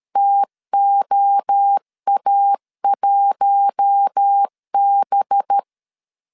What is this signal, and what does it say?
Morse code, heard as a keyed tone, about 13 wpm, 800 Hz. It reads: TOA1B
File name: CW / DET / 1 6 1500C S A 12wpm 3440